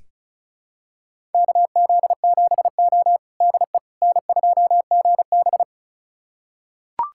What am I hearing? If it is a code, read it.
Morse code, heard as a keyed tone, about 36 wpm, 700 Hz. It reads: KZ7O DE N2GB
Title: USA Callsign DE Callsign - 4 Characters - 35wpm